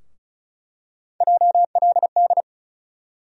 Morse code, heard as a keyed tone, about 36 wpm, 700 Hz. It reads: JLD
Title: USA Call Sign Postfixes - 3 Letters 35wpm